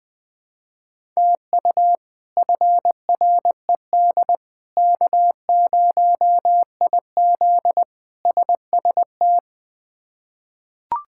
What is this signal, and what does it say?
Morse code, heard as a keyed tone, about 20 wpm, 700 Hz. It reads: TU FRED K0IZ SST
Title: K1USN SST QSO Practice - 20wpm